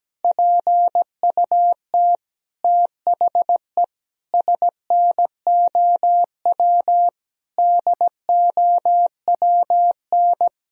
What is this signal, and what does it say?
Morse code, heard as a keyed tone, about 17 wpm, 700 Hz. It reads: PUT THE SNOW DOWN
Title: Sentences from Top 1000 Words - Review Part 3 - 17wpm